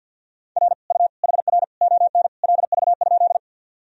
Morse code, heard as a keyed tone, about 49 wpm, 700 Hz. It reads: R U HR ON LV?
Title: Sentences from Top 400 Words and Abbreviations 50wpm